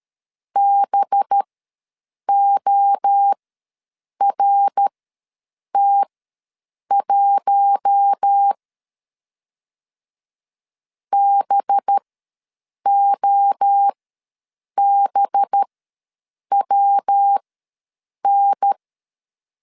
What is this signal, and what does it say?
Morse code, heard as a keyed tone, about 13 wpm, spaced as for 7 wpm, 800 Hz. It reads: BORT1 BOBWN
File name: 5 LB ws 12wpm